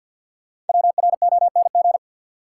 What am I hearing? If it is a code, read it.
Morse code, heard as a keyed tone, about 52 wpm, 700 Hz. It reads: WRONG
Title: Sets of 5 Words Encore - New Word Drill - 50wpm